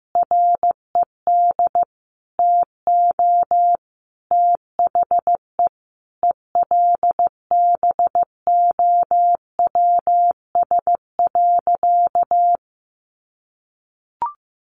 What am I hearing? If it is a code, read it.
Morse code, heard as a keyed tone, about 15 wpm, 700 Hz. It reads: RED TO THE ELBOWS.
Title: A Sound Of Thunder 15wpm